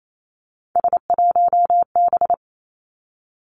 Morse code, heard as a keyed tone, about 28 wpm, 700 Hz. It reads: S1B
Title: Three Challenging Letter-Number Combinations 28wpm